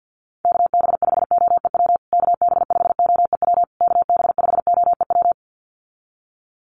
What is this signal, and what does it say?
Morse code, heard as a keyed tone, about 50 wpm, 700 Hz. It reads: KB5OEW KB5OEW KB5OEW
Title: USA Call Signs Repeated 3x 50wpm